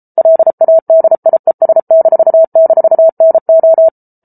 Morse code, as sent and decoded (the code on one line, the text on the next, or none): .-.. .- -.. .. ... -....- -....- -. ---
LADIES--NO